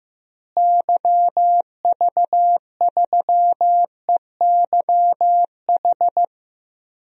Morse code, heard as a keyed tone, about 15 wpm, 700 Hz. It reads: YV3EYH
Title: DXCC-100 Call Signs - 6 Characters - Repeated 2x - 15wpm